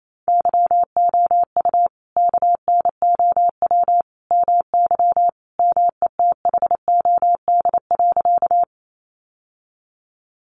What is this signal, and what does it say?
Morse code, heard as a keyed tone, about 28 wpm, 700 Hz. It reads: YOU KNOW MY METHOD.
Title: Sentences from Top 600 Words 28wpm